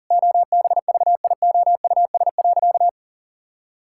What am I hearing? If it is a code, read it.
Morse code, heard as a keyed tone, about 40 wpm, 700 Hz. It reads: OBVIOUS.
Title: Sentences from Top 1600 Words 40wpm